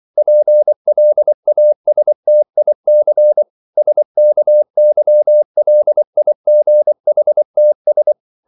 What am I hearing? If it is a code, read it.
Morse code, heard as a keyed tone, about 24 wpm, 600 Hz. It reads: PLASTIC SKYLIGHTS